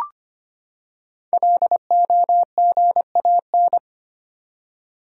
Morse code, heard as a keyed tone, about 25 wpm, 700 Hz. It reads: LOGAN
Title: USA Common Names - Max 5 Letters - 25wpm